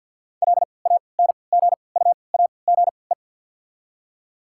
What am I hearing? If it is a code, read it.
Morse code, heard as a keyed tone, about 53 wpm, spaced as for 29 wpm, 700 Hz. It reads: LANGUAGE
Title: 3x Character Spacing - Top 500 Words - 50wpm